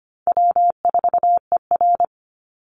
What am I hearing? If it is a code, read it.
Morse code, heard as a keyed tone, about 25 wpm, 700 Hz. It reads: W4ER
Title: Easy Call Signs - 25wpm